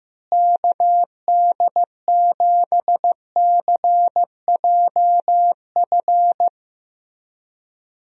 Morse code, heard as a keyed tone, about 15 wpm, 700 Hz. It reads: KD7CJF
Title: USA Call Signs - 6 Characters - Repeated 3x - 15wpm